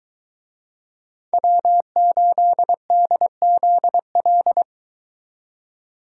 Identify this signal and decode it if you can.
Morse code, heard as a keyed tone, about 22 wpm, 700 Hz. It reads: W8DZL